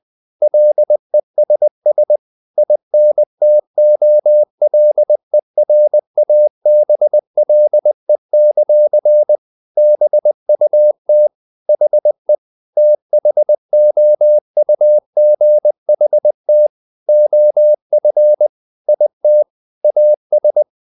Morse code, heard as a keyed tone, about 20 wpm, 600 Hz. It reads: LESS INTOLERABLE; BUT HE THOUGHT OF IT AS
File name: the old man and the sea / the old man and the sea part 1 20wpm